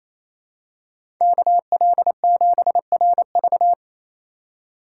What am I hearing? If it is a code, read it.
Morse code, heard as a keyed tone, about 28 wpm, 700 Hz. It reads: KL7RV
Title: USA Call Signs - 5 Characters - 4x Character Spacing - 28wpm